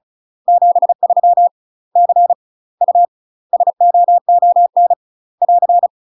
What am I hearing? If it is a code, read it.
Morse code, heard as a keyed tone, about 36 wpm, 700 Hz. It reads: 73 C U SOON <AR>
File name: UK MASTER QSOs TRAINING / 35WPM / UK MASTER QSOs 0043